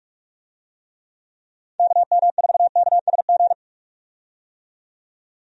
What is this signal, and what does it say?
Morse code, heard as a keyed tone, about 46 wpm, 700 Hz. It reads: KM4KSG
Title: USA Call Signs - 6 Characters - 45wpm